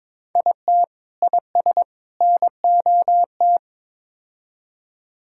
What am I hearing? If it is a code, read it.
Morse code, heard as a keyed tone, about 22 wpm, 700 Hz. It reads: IT IS NOT
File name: practice / 22wpm / Build A Phrase - 5-Word Phrases - 22wpm